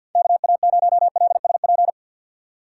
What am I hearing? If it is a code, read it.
Morse code, heard as a keyed tone, about 53 wpm, 700 Hz. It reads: KA0LSP
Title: USA Call Signs - 6 Characters - 4x Character Spacing - 50wpm